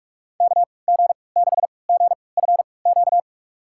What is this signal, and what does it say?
Morse code, heard as a keyed tone, about 42 wpm, spaced as for 25 wpm, 700 Hz. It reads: KG6GFQ
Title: USA Call Signs - 6 Characters - 3x Character Spacing - 45wpm